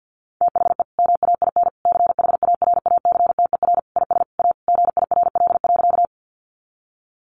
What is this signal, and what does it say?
Morse code, heard as a keyed tone, about 50 wpm, 700 Hz. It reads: THE MAIN CHARACTER IS A GIRL.
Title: Additional Easy Phrases 50wpm